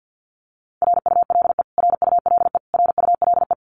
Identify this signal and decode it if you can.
Morse code, heard as a keyed tone, about 50 wpm, 700 Hz. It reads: RULE RULE RULE
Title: Top 500 Words Repeated 50wpm